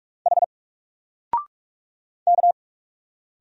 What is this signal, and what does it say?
Morse code, heard as a keyed tone, about 46 wpm, 700 Hz. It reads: H K
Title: Single Letter-Number 45wpm